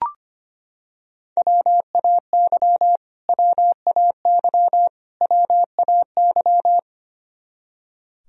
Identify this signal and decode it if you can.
Morse code, heard as a keyed tone, about 25 wpm, 700 Hz. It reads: WAY WAY WAY
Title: Top 100 Words Repeated 25wpm